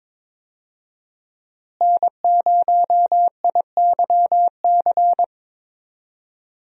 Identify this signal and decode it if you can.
Morse code, heard as a keyed tone, about 22 wpm, 700 Hz. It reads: N0IYC
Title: USA Call Signs - 5 Characters - Repeated 3x - 22wpm